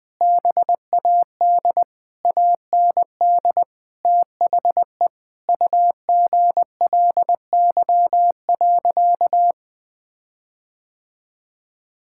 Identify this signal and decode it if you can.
Morse code, heard as a keyed tone, about 20 wpm, 700 Hz. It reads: BAD AND THE UGLY.
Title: Sentences with 7-Word Phrases - 20wpm